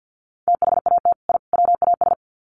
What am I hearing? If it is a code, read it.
Morse code, heard as a keyed tone, about 50 wpm, 700 Hz. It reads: THAT I WAS
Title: Sets of 3 Words 50wpm